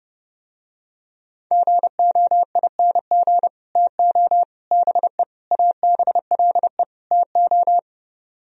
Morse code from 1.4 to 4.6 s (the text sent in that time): GOING TO